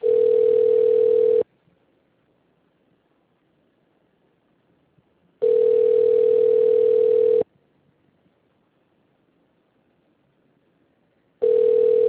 Ringback tone (North America), 440 + 480 Hz.